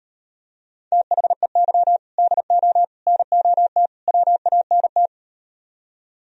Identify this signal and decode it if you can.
Morse code, heard as a keyed tone, about 38 wpm, 700 Hz. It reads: THEY DO NOT WANT